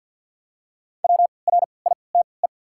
Morse code, heard as a keyed tone, about 53 wpm, 690 Hz. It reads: W R I T E
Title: 3x Character Spacing - Top 200 Words - 50wpm